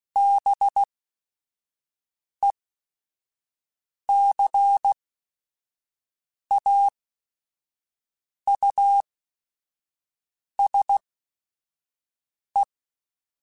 Morse code, heard as a keyed tone, about 16 wpm, spaced as for 5 wpm, 800 Hz. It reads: BECAUSE
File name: Chuckles / Chuckles-mp3s / SoftWords @5